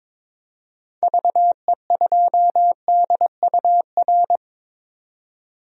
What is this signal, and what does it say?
Morse code, heard as a keyed tone, about 22 wpm, 700 Hz. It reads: VE2DUR
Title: DXCC-100 Call Signs - 6 Characters - 22wpm